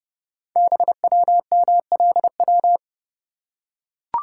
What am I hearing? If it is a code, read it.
Morse code, heard as a keyed tone, about 29 wpm, 700 Hz. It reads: BWMLW
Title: Code Groups - 5 Letters - 30wpm